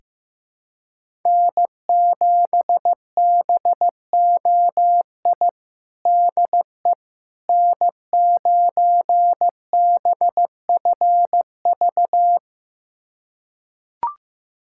Morse code, heard as a keyed tone, about 15 wpm, 700 Hz. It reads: N7BOI DE N9BFV
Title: USA Callsign DE Callsign - 5 Characters - 15wpm